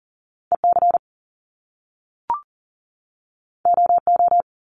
Morse code, heard as a keyed tone, about 40 wpm, 700 Hz. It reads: EC OO
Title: Top Two Letter Combinations in Words 40wpm